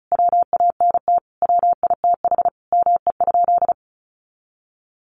Morse code, heard as a keyed tone, about 34 wpm, 700 Hz. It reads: WANT WITH ME?
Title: Sentences from Top 200 Words 35wpm